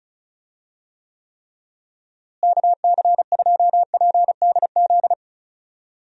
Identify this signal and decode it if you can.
Morse code, heard as a keyed tone, about 36 wpm, 700 Hz. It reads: KC2PDZ
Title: USA Call Signs - 6 Characters - Repeated 3x - 35wpm